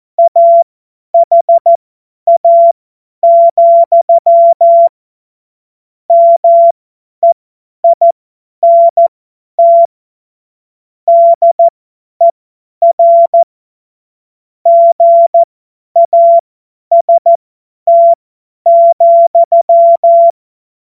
Morse code, heard as a keyed tone, about 14 wpm, spaced as for 10 wpm, 680 Hz. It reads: AHA, MEINT DER GAST,